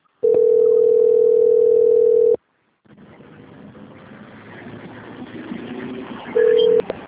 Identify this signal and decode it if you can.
Ringback tone (North America), 440 + 480 Hz.
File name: out- - - - - .31355